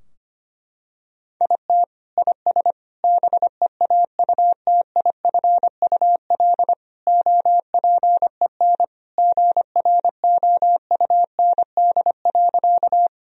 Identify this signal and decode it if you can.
Morse code, heard as a keyed tone, about 25 wpm, 700 Hz. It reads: IT IS BEAUTIFUL OPEN GROUND.
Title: Sentences from Top 1000 Words - Review Part 2 - 25wpm